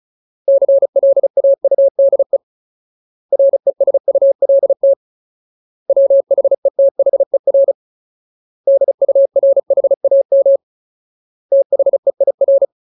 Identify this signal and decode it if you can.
Morse code, heard as a keyed tone, about 34 wpm, 550 Hz. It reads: CLAUDE RESULT WHETHER DURHAM THEIR
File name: Ultimate Word List / 35WPM Ultimate